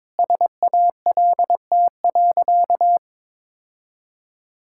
Morse code, heard as a keyed tone, about 22 wpm, 700 Hz. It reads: SALT.
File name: Sentences with 7-Word Phrases - 22wpm